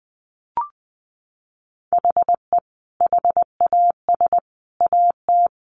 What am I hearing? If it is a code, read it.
Morse code, heard as a keyed tone, about 20 wpm, 700 Hz. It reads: HE HAS AT
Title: Sentences from Top 1500 Words 20wpm